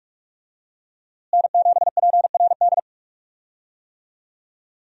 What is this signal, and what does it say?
Morse code, heard as a keyed tone, about 43 wpm, 700 Hz. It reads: N7PRD
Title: USA Call Signs - 5 Characters - Repeated 2x - 45wpm